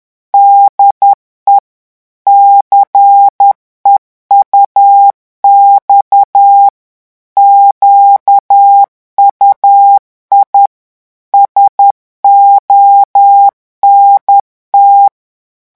Morse code, heard as a keyed tone, about 10 wpm, 800 Hz. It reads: DE CEUX QUI SONT
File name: revision9 citations humoristiques vitesse 10 mots minute